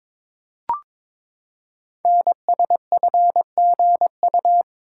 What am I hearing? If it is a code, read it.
Morse code, heard as a keyed tone, about 22 wpm, 700 Hz. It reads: NSFGU
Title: Code Groups - 5 Letters - 22wpm